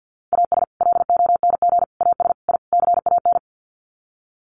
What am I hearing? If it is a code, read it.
Morse code, heard as a keyed tone, about 50 wpm, 700 Hz. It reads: AS LONG AS I CAN
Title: Sets of 5 Words Encore2 50wpm